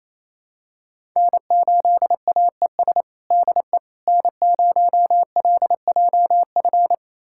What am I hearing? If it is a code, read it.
Morse code, heard as a keyed tone, about 28 wpm, 700 Hz. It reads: N8AES DE N0LJF
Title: USA Callsign DE Callsign - 5 Characters - 28wpm